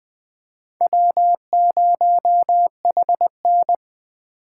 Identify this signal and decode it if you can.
Morse code, heard as a keyed tone, about 21 wpm, 700 Hz. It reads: W0HN
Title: Speed Racing - USA 1x2 Call Signs 17wpm